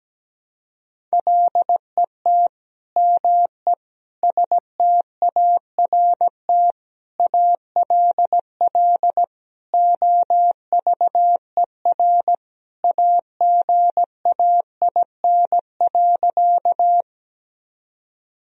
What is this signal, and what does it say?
Morse code, heard as a keyed tone, about 17 wpm, 700 Hz. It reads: LET ME START ALL OVER AGAIN.
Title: Sentences from Top 1000 Words - Review Part 1 - 17wpm